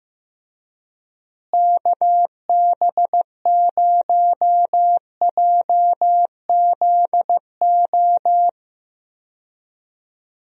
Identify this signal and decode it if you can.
Morse code, heard as a keyed tone, about 15 wpm, 700 Hz. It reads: KB0JZO
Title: USA Call Signs - 6 Characters - Repeated 2x - 15wpm